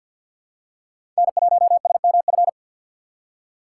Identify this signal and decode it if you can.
Morse code, heard as a keyed tone, about 51 wpm, 700 Hz. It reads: N1SMF
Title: USA Call Signs - 5 Characters - 3x Character Spacing - 50wpm